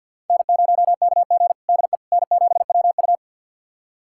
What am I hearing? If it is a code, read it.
Morse code, heard as a keyed tone, about 47 wpm, 700 Hz. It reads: N0KG DE N7WU
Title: USA Callsign DE Callsign - 4 Characters - 50wpm